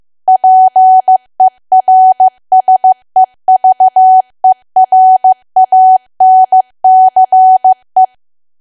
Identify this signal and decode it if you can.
Morse code, heard as a keyed tone, about 15 wpm, 750 Hz. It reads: PERSEVERANCE
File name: cwa / W1AW-Level2 / 15WPM